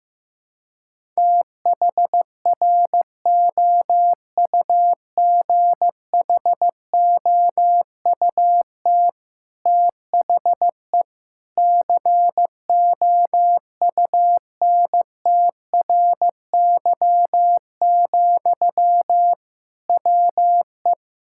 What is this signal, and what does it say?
Morse code, heard as a keyed tone, about 15 wpm, 700 Hz. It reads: THROUGHOUT THE COUNTRY, WE
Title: Sentences from Top 1000 Words - Review Part 5 - 15wpm